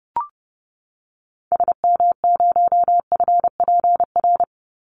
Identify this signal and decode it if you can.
Morse code, heard as a keyed tone, about 30 wpm, 700 Hz. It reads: SM0FPR
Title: Call Signs Repeated 2x - 30wpm